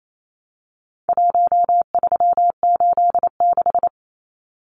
Morse code, heard as a keyed tone, about 28 wpm, 700 Hz. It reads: 1386
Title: Numbers - Contest Style - 4-Digit 28wpm